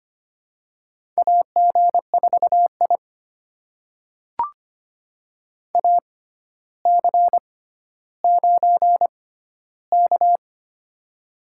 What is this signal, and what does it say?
Morse code, heard as a keyed tone, about 25 wpm, spaced as for 13 wpm, 700 Hz. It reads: AG4I AC9K
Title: USA Call Signs - 4 Characters - 6x Character Spacing - 25wpm